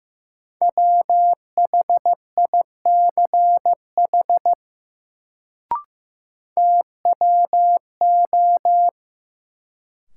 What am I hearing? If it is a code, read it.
Morse code, heard as a keyed tone, about 15 wpm, 700 Hz. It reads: WHICH TWO
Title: Sentences from Top 100 Words - New Word Drill - 15wpm